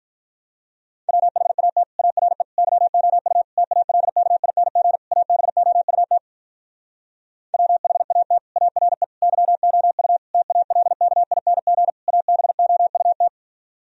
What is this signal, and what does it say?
Morse code, heard as a keyed tone, about 46 wpm, 700 Hz. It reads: WHAT ARE YOU TALKING ABOUT WHAT ARE YOU TALKING ABOUT